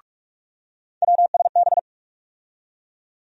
Morse code, heard as a keyed tone, about 46 wpm, 700 Hz. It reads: WSB
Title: USA Call Sign Postfixes - 3 Letters 45wpm